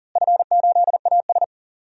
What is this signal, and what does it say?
Morse code, heard as a keyed tone, about 37 wpm, 700 Hz. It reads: F8AS
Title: DXCC-100 Call Signs - 4 Characters - Repeated 3x - 40wpm